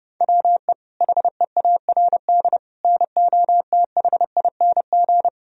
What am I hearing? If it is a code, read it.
Morse code, heard as a keyed tone, about 30 wpm, 700 Hz. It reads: WE HEARD NOTHING
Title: Sentences from Top 1000 Words - Review Part 3 - 30wpm